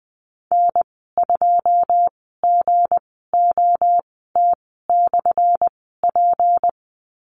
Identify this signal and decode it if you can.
Morse code, heard as a keyed tone, about 20 wpm, spaced as for 15 wpm, 700 Hz. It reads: N2GOT/P
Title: 179 Lesson 29g - Slash - Call Signs 20wpm